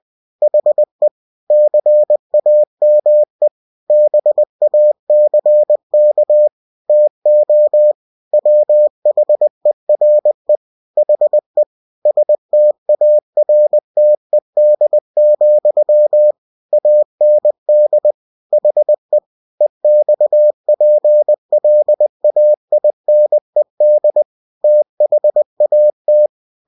Morse code, heard as a keyed tone, about 20 wpm, 600 Hz. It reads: HE CAME BACK TO WHERE HE STARTED, AND HE EXPLAINED THAT
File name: winnie-20wpm